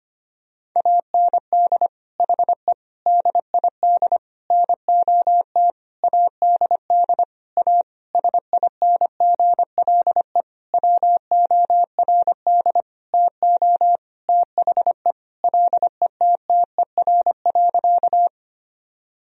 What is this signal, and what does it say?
Morse code, heard as a keyed tone, about 25 wpm, 700 Hz. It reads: AND HE DID NOT ADD A SINGLE WORD TO THE LETTER.
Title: Sentences from Top 1000 Words - Review Part 1 - 25wpm